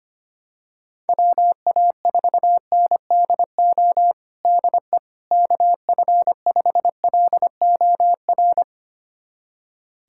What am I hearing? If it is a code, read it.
Morse code, heard as a keyed tone, about 25 wpm, 700 Hz. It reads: WA4NDO DE KF5LOR
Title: USA Callsign DE Callsign 25wpm